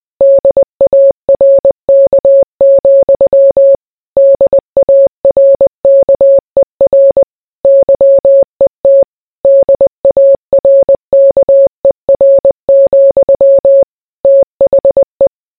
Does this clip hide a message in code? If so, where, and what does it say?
Yes: Morse code from 0.2 to 15.3 s, heard as a keyed tone, about 20 wpm, 550 Hz. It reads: DARK, DARKER YET DARKER, THE